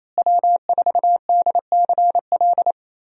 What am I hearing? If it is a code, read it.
Morse code, heard as a keyed tone, about 28 wpm, 700 Hz. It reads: W4DCL
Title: USA Call Signs - 5 Characters - Repeated 2x - 28wpm